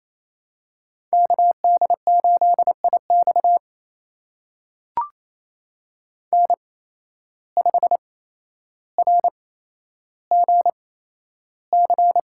Morse code, heard as a keyed tone, about 28 wpm, spaced as for 12 wpm, 700 Hz. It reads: KD8IX N5RGC
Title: USA Call Signs - 5 Characters - 8x Character Spacing - 28wpm